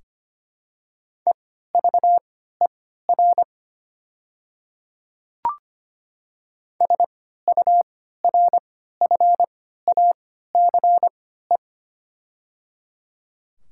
Morse code, heard as a keyed tone, about 25 wpm, spaced as for 14 wpm, 700 Hz. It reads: EVER SURFACE
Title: 3x Character Spacing - Top 500 Words - 25wpm